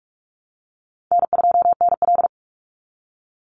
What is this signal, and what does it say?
Morse code, heard as a keyed tone, about 44 wpm, 700 Hz. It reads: N2NL
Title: Call Signs Repeated 3x - 45wpm